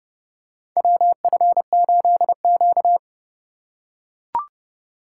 Morse code, heard as a keyed tone, about 30 wpm, 700 Hz. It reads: WF8Q
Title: USA Call Signs - 4 Characters - 3x Character Spacing - 30wpm